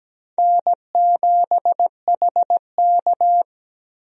Morse code, heard as a keyed tone, about 17 wpm, 700 Hz. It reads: N7HK
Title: USA Call Signs - 4 Characters - Repeated 2x - 17wpm